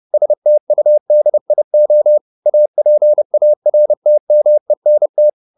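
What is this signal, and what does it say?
Morse code, heard as a keyed tone, about 30 wpm, 600 Hz. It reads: STUDIO APARTMENT
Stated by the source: Morse code at 29 wpm, sending Ken Follett's novel Never in English, from CW med Gnister.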